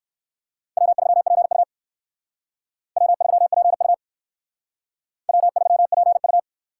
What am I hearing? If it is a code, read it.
Morse code, heard as a keyed tone, about 56 wpm, 700 Hz. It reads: W3PU W3PU W3PU
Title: Speed Racing - USA 1x2 Call Signs 45wpm